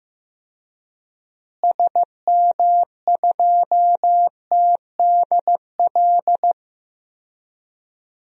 Morse code, heard as a keyed tone, about 15 wpm, 700 Hz. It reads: SM2TDL
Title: DXCC-100 Call Signs - 6 Characters - Repeated 3x - 15wpm